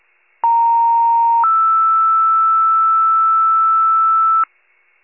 Two-tone page, heard as a one-second tone, then a three-second tone, about 930 then 1320 Hz, with noise. It is heard from 0.4 to 4.4 s.